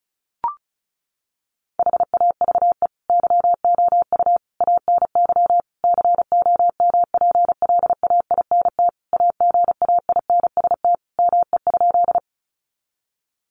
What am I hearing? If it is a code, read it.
Morse code, heard as a keyed tone, about 34 wpm, 700 Hz. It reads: HAVE YOU ANY COMPLAINT AGAINST ME?
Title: Sentences from Top 1400 Words 35wpm